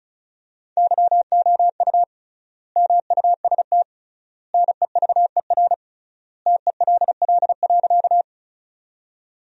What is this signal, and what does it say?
Morse code, heard as a keyed tone, about 36 wpm, 700 Hz. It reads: YOU MUST NEVER TELL.
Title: 3x Word-Spacing - Sentences from Top 300 Words - 35wpm